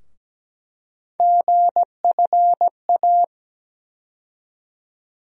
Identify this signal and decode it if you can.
Morse code, heard as a keyed tone, about 17 wpm, 700 Hz. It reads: GFA
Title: USA Call Sign Postfixes - 3 Letters 17wpm